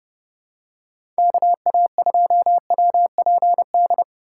Morse code, heard as a keyed tone, about 32 wpm, 700 Hz. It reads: KA2WPD
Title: USA Call Signs - 6 Characters - 6x Character Spacing - 30wpm